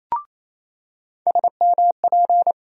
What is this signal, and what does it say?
Morse code, heard as a keyed tone, about 28 wpm, 700 Hz. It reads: SMP
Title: USA Call Sign Postfixes - 3 Letters 28wpm